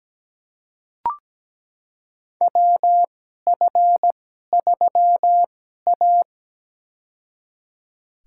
Morse code, heard as a keyed tone, about 17 wpm, spaced as for 12 wpm, 700 Hz. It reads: WF3A
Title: USA Call Signs - 4 Characters - 2x Character Spacing - 17wpm